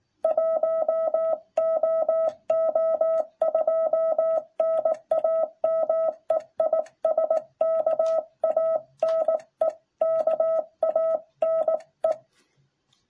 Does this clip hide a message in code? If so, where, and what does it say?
Yes: Morse code from 0.2 to 12.1 s, heard as a keyed tone, about 17 wpm, 660 Hz. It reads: 1OO2 NAME IS KANE KANE